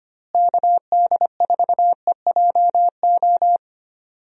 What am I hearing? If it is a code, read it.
Morse code, heard as a keyed tone, about 25 wpm, 700 Hz. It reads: KD4EJO
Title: USA Call Signs - 6 Characters - Repeated 3x - 25wpm